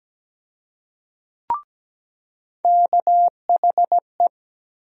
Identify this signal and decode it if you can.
Morse code, heard as a keyed tone, about 17 wpm, 700 Hz. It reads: KHE
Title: USA Call Sign Postfixes - 3 Letters 17wpm